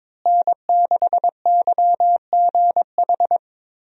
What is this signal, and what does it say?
Morse code, heard as a keyed tone, about 22 wpm, 700 Hz. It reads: N6YGH
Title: USA Call Signs - 5 Characters - Repeated 3x - 22wpm